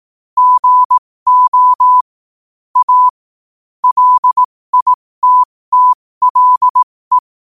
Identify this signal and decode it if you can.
Morse code, heard as a keyed tone, about 18 wpm, spaced as for 15 wpm, 1000 Hz. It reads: GO A LITTLE
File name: qotdfiles / 15wpm